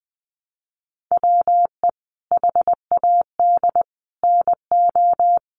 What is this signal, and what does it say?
Morse code, heard as a keyed tone, about 20 wpm, 700 Hz. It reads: WE HAD NO
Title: Sentences from Top 1200 Words 20wpm